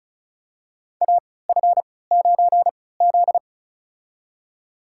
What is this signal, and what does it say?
Morse code, heard as a keyed tone, about 36 wpm, spaced as for 20 wpm, 700 Hz. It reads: AF9Z
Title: USA Call Signs - 4 Characters - 3x Character Spacing - 35wpm